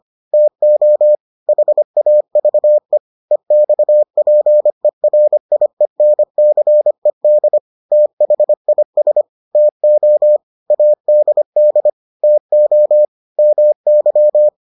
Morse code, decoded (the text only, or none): TO HAVE EXPERIENCED THIS TO ADD TO MY